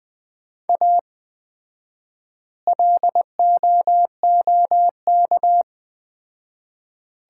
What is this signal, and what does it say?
Morse code, heard as a keyed tone, about 21 wpm, 700 Hz. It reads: A LOOK
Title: 4x Word-Spacing - Sets of 3 Words - 20wpm